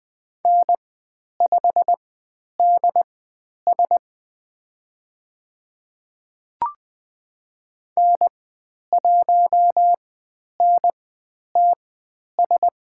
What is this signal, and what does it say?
Morse code, heard as a keyed tone, about 20 wpm, spaced as for 10 wpm, 700 Hz. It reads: N5DS N1NTS
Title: 109 Lesson 18g - Letter m - Call Signs 20wpm